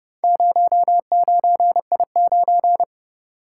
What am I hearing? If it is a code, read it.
Morse code, heard as a keyed tone, about 29 wpm, 700 Hz. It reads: 09I9